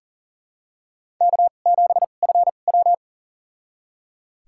Morse code, heard as a keyed tone, about 38 wpm, spaced as for 28 wpm, 700 Hz. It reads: K7FW
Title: USA Call Signs - 4 Characters - 2x Character Spacing - 40wpm